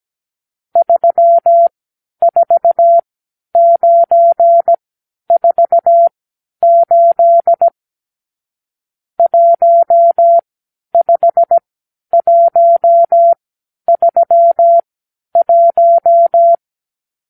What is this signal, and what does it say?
Morse code, heard as a keyed tone, about 17 wpm, spaced as for 10 wpm, 680 Hz. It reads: 34948 15131
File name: arrl / Review (1-0)